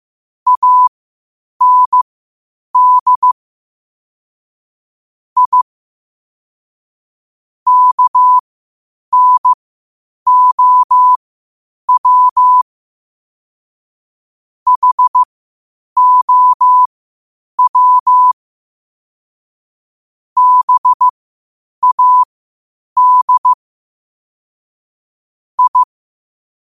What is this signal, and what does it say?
Morse code, heard as a keyed tone, about 15 wpm, spaced as for 8 wpm, 1000 Hz. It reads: AND I KNOW HOW BAD I
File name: qotdfiles / 8wpm